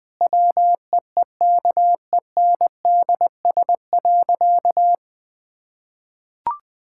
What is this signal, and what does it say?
Morse code, heard as a keyed tone, about 19 wpm, 700 Hz. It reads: WEEKENDS.
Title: Difficult Sentences with Hints - 20wpm